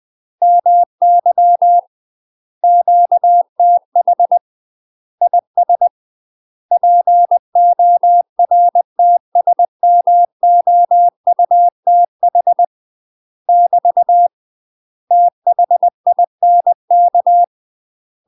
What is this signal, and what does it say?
Morse code, heard as a keyed tone, about 20 wpm, 700 Hz. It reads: MY QTH IS PORTSMOUTH <BT> THINK
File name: UK MASTER QSOs TRAINING / 20WPM / UK MASTER QSOs 0064